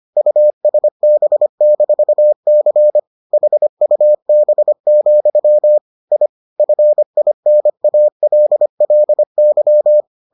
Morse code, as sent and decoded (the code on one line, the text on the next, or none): ..- ... -... -....- -.-. / .... ..- -... --..-- / .. / ..-. .. -. .- .-.. .-.. -.--
USB-C HUB, I FINALLY